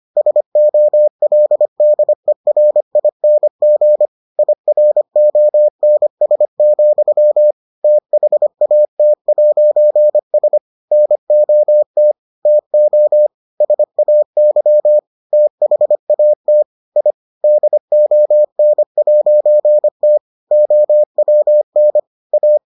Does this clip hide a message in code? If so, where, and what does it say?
Yes: Morse code from 0.2 to 22.6 s, heard as a keyed tone, about 25 wpm, 600 Hz. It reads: SOLDERING IRONS, THAT'S NOT TO SAY THAT I DON'T OWN A